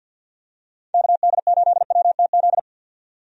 Morse code, heard as a keyed tone, about 45 wpm, 700 Hz. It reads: KD8WTZ
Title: USA Call Signs - 6 Characters - 50wpm